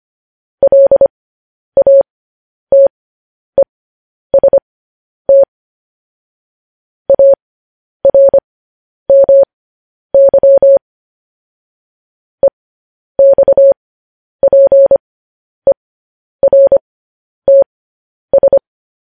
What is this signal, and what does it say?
Morse code, heard as a keyed tone, about 25 wpm, spaced as for 10 wpm, 550 Hz. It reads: LATEST ARMY EXPERTS
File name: podcast / News headlines-1x-25x10